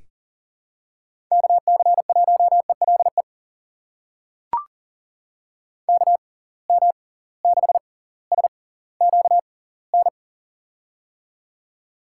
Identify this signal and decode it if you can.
Morse code, heard as a keyed tone, about 40 wpm, spaced as for 20 wpm, 700 Hz. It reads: KC1ELE KM6SQN
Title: USA Call Signs - 6 Characters - 6x Character Spacing - 40wpm